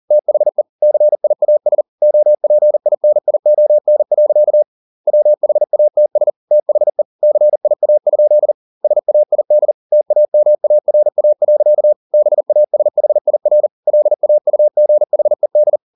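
Morse code, heard as a keyed tone, about 40 wpm, 600 Hz. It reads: THE CIAS OPINION. WHATS THE CIA? SAID TAMARA. BASHIR LAUGHED
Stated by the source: Morse code at 40 wpm, practice text, CW med Gnister.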